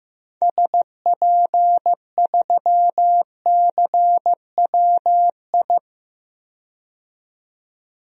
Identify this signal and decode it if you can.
Morse code, heard as a keyed tone, about 15 wpm, 700 Hz. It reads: SP3CWI
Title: DXCC-100 Call Signs - 6 Characters - 15wpm